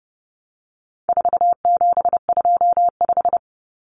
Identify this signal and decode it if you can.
Morse code, heard as a keyed tone, about 30 wpm, 700 Hz. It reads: 4725
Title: Numbers - 4-Digit 30wpm